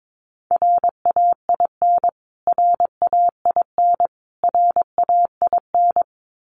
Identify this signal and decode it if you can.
Morse code, heard as a keyed tone, about 22 wpm, 700 Hz. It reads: RAIN RAIN RAIN
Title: Top 500 Words Repeated 22wpm